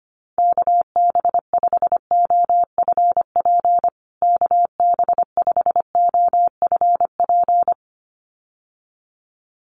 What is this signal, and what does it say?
Morse code, heard as a keyed tone, about 25 wpm, 700 Hz. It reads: KB5OFP KB5OFP
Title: USA Call Signs Repeated 2x 25wpm